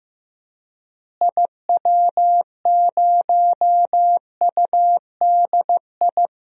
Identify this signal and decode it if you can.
Morse code, heard as a keyed tone, about 16 wpm, 700 Hz. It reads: IW0UDI
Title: DXCC-100 Call Signs - 6 Characters - Repeated 2x - 15wpm